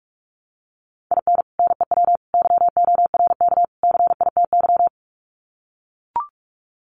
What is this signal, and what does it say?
Morse code, heard as a keyed tone, about 44 wpm, 700 Hz. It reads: IN NEW YORK CITY
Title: Sets of 4 Words 45wpm